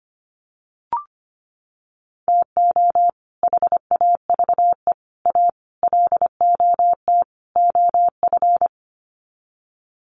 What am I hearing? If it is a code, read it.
Morse code, heard as a keyed tone, about 25 wpm, 700 Hz. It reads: TO HAVE A LOT OF
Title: Sets of 5 Words Encore 25wpm